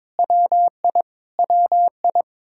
Morse code, heard as a keyed tone, about 22 wpm, 700 Hz. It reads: WI WI
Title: K1USN SST QSO Practice 22wpm